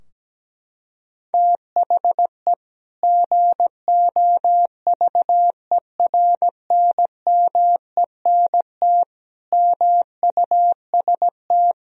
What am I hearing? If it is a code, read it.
Morse code, heard as a keyed tone, about 17 wpm, 700 Hz. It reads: THE GOVERNMENT MUST
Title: Sentences from Top 1000 Words - Review Part 3 - 17wpm